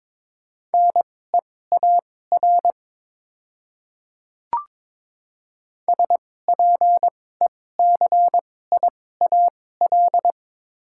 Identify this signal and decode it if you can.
Morse code, heard as a keyed tone, about 22 wpm, spaced as for 16 wpm, 690 Hz. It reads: NEAR SPECIAL
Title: 2x Character Spacing - Top 500 Words - 22wpm